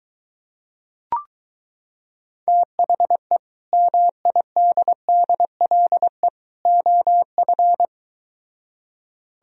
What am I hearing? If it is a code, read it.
Morse code, heard as a keyed tone, about 23 wpm, 700 Hz. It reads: THE MIDDLE OF